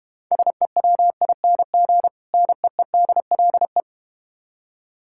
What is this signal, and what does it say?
Morse code, heard as a keyed tone, about 32 wpm, 700 Hz. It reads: SEWING NEEDLE